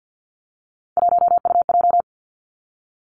Morse code, heard as a keyed tone, about 50 wpm, 700 Hz. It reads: 1UJ
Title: Three Challenging Letter-Number Combinations 50wpm